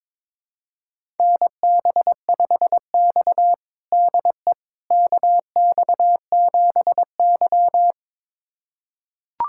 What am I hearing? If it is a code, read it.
Morse code, heard as a keyed tone, about 22 wpm, 700 Hz. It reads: NB5X DE KX7Y
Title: USA Callsign DE Callsign - 4 Characters - 22wpm